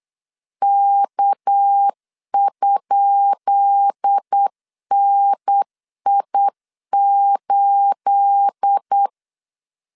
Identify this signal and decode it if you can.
Morse code, heard as a keyed tone, about 8 wpm, 800 Hz. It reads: K?NI8
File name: LEZIONI / 10 SIGNS / 10B INT 8wpm 3534